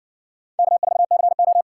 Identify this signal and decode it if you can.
Morse code, heard as a keyed tone, about 60 wpm, 700 Hz. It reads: K4CO